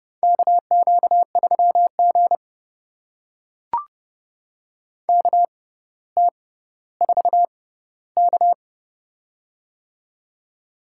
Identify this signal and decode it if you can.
Morse code, heard as a keyed tone, about 29 wpm, spaced as for 15 wpm, 700 Hz. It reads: KQ3G KT4K
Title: USA Call Signs - 4 Characters - 6x Character Spacing - 30wpm